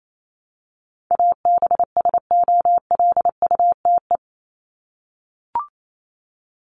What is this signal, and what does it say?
Morse code, heard as a keyed tone, about 28 wpm, 700 Hz. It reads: ABSOLUTE
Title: LY Suffix - Top 100 Words with Base Words 28wpm